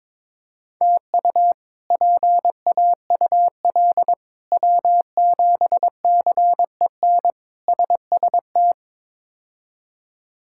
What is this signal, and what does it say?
Morse code, heard as a keyed tone, about 22 wpm, 700 Hz. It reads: TU PAUL W7CEN SST
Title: K1USN SST QSO Practice - 22wpm